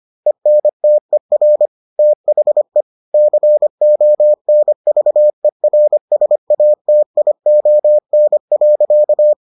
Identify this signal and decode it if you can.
Morse code, heard as a keyed tone, about 25 wpm, 600 Hz. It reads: ENTER THE CONVERSATION.